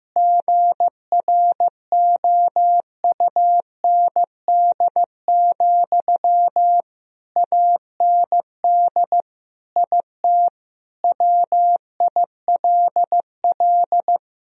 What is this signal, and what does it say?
Morse code, heard as a keyed tone, about 15 wpm, 700 Hz. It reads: GROUND, AND IT WILL
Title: Sentences from Top 1000 Words - Review Part 2 - 15wpm